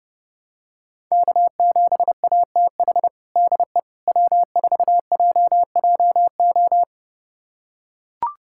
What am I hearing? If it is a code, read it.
Morse code, heard as a keyed tone, about 30 wpm, 700 Hz. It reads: K7ATH DE W4JJO
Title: USA Callsign DE Callsign - 5 Characters - 30wpm